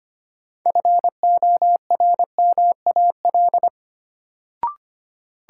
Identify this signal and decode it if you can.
Morse code, heard as a keyed tone, about 25 wpm, 700 Hz. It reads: FORMAL
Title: Sentences from Top 1300 Words - New Word Drill - 25wpm